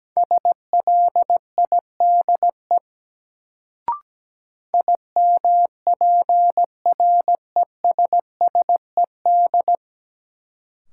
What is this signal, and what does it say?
Morse code, heard as a keyed tone, about 17 wpm, 700 Hz. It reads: SLIDE IMPRESSED
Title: Sentences from Top 1600 Words - New Word Drill Alt - 17wpm